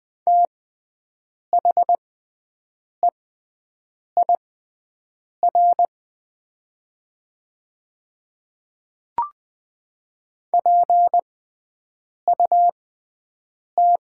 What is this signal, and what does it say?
Morse code, heard as a keyed tone, about 20 wpm, spaced as for 7 wpm, 700 Hz. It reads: THEIR PUT
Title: 6x Character Spacing - Top 200 Words - 20wpm